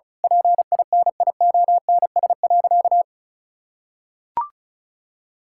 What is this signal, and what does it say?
Morse code, heard as a keyed tone, about 36 wpm, 700 Hz. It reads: PINIONS.
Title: Pangram Challenge v2 - 35wpm